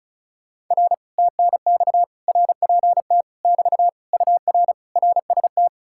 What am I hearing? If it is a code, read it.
Morse code, heard as a keyed tone, about 46 wpm, 700 Hz. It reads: R TNX RPT <BT> UR RST